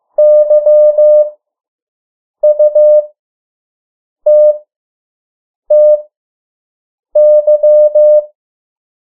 Morse code, heard as a keyed tone, about 15 wpm, spaced as for 6 wpm, 600 Hz. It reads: YUTTY
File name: morse mp3 / 03 urok 2